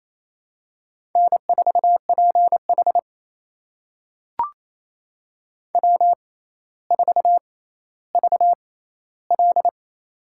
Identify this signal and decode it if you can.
Morse code, heard as a keyed tone, about 28 wpm, spaced as for 14 wpm, 700 Hz. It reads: N4PH W4VL
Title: USA Call Signs - 4 Characters - 6x Character Spacing - 28wpm